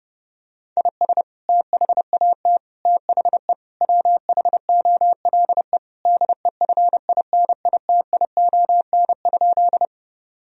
Morse code, heard as a keyed tone, about 30 wpm, 700 Hz. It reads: IS THAT THE WHOLE DEFINITION?
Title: Sentences from Top 1000 Words - Review Part 4 - 30wpm